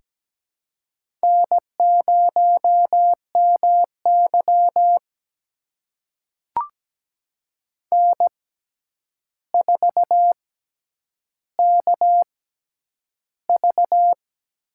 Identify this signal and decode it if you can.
Morse code, heard as a keyed tone, about 17 wpm, spaced as for 9 wpm, 700 Hz. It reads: N0MY N4KV
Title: USA Call Signs - 4 Characters - 6x Character Spacing - 17wpm